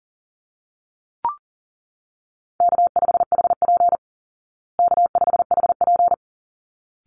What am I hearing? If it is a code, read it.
Morse code, heard as a keyed tone, about 40 wpm, 700 Hz. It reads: K5HP K5HP
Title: Call Signs Repeated 3x - 40wpm